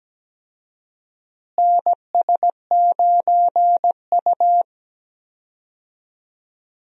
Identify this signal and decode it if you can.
Morse code, heard as a keyed tone, about 17 wpm, 700 Hz. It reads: NS9U
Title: USA Call Signs - 4 Characters - Repeated 2x - 17wpm